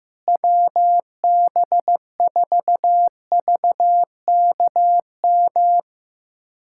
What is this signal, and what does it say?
Morse code, heard as a keyed tone, about 15 wpm, 700 Hz. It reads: WB4VKM
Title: USA Callsign DE Callsign - 6 Characters - 15wpm